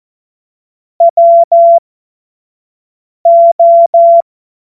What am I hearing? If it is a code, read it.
Morse code, heard as a keyed tone, about 14 wpm, spaced as for 5 wpm, 680 Hz. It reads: WO
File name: Morsekurs / Witze / Witz 22. November 2022 25 F70